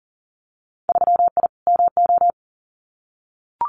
Morse code, heard as a keyed tone, about 40 wpm, 700 Hz. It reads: 3I MO
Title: Winter Field Day Exchanges 40wpm